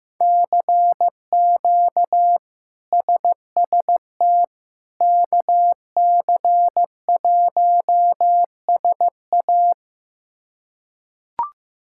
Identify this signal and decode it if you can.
Morse code, heard as a keyed tone, about 15 wpm, 700 Hz. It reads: CQ SST KC1SA
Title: K1USN SST QSO Practice - 15wpm